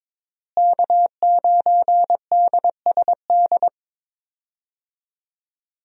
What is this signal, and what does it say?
Morse code, heard as a keyed tone, about 22 wpm, 700 Hz. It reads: K9DSD
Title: USA Call Signs - 5 Characters - Repeated 3x - 22wpm